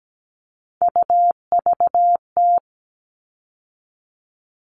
Morse code, heard as a keyed tone, about 17 wpm, 700 Hz. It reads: UVT
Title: All Cut Numbers - 3-Digit 17wpm